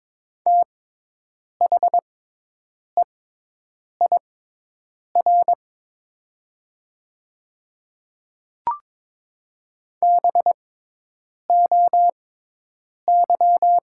Morse code, heard as a keyed tone, about 22 wpm, spaced as for 8 wpm, 700 Hz. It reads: THEIR BOY
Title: 6x Character Spacing - Top 300 Words - 22wpm